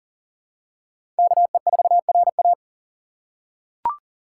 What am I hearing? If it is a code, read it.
Morse code, heard as a keyed tone, about 40 wpm, 700 Hz. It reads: KE4RA
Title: USA Call Signs - 2x Character Spacing - 40wpm